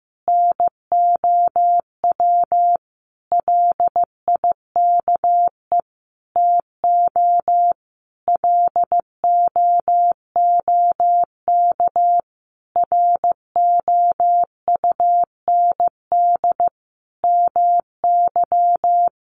Morse code, heard as a keyed tone, about 15 wpm, 700 Hz. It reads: NOW LIKE TO LOOK ROUND MY
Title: Sentences from Top 200 Words 15wpm